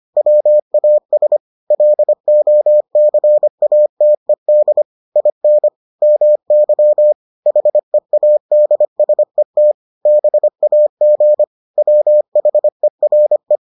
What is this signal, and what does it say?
Morse code, heard as a keyed tone, about 25 wpm, 600 Hz. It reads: WAS LOCATED IN MY HEADSET BAG WHERE